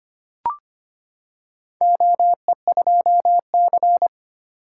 Morse code, heard as a keyed tone, about 25 wpm, 700 Hz. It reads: OE2C
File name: practice / 25wpm / Four Letter-Number Combinations - 25wpm